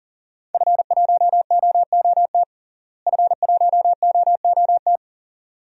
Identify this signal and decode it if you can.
Morse code, heard as a keyed tone, about 40 wpm, 700 Hz. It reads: F1OOT F1OOT
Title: DXCC-100 Call Signs - 5 Characters - Repeated 2x - 40wpm